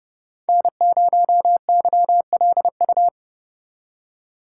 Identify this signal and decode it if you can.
Morse code, heard as a keyed tone, about 30 wpm, 700 Hz. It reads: N0YLU
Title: USA Call Signs - 5 Characters - 3x Character Spacing - 30wpm